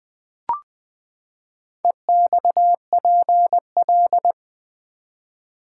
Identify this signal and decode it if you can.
Morse code, heard as a keyed tone, about 20 wpm, 700 Hz. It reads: EXPL
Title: Build A Word - 4 Letters At A Time - 12-Letter Words - 20wpm